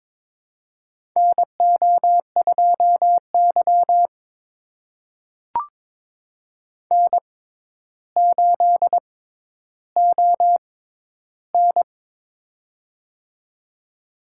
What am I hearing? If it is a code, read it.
Morse code, heard as a keyed tone, about 23 wpm, spaced as for 11 wpm, 700 Hz. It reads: NO2Y N8ON
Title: USA Call Signs - 6x Character Spacing - 22wpm